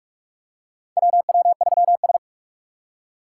Morse code, heard as a keyed tone, about 44 wpm, 700 Hz. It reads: WW3S
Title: USA Call Signs - 4 Characters - 6x Character Spacing - 45wpm